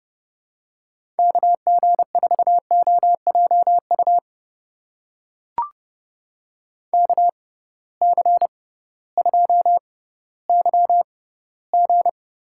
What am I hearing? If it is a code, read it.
Morse code, heard as a keyed tone, about 30 wpm, spaced as for 16 wpm, 700 Hz. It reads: KG4OJU KC2YG
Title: USA Call Signs - 6 Characters - 6x Character Spacing - 30wpm